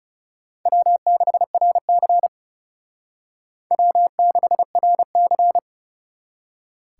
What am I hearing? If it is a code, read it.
Morse code, heard as a keyed tone, about 38 wpm, 700 Hz. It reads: W6RC W6RC